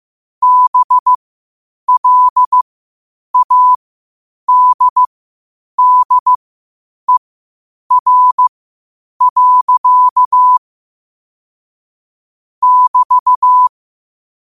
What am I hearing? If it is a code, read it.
Morse code, heard as a keyed tone, about 15 wpm, spaced as for 8 wpm, 1000 Hz. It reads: BLADDER. <BT>
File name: qotdfiles / 8wpm